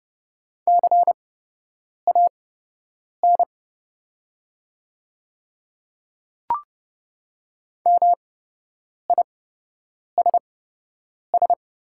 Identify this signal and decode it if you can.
Morse code, heard as a keyed tone, about 30 wpm, spaced as for 8 wpm, 700 Hz. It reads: CAN MISS
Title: 8x Character Spacing - Top 500 Words - 30wpm